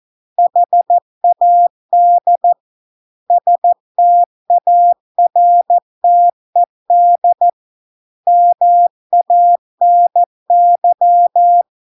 Morse code, heard as a keyed tone, about 14 wpm, 700 Hz. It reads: HAD STARTED MANY